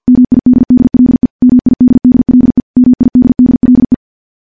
Morse code, heard as a keyed tone, about 50 wpm, 250 Hz. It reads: MIDDLE MIDDLE MIDDLE
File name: morse-training-50WPM